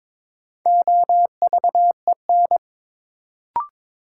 Morse code, heard as a keyed tone, about 23 wpm, 700 Hz. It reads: OVEN
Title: Sentences from Top 1000 Words - Review Part 4 - New Word Drill - 22wpm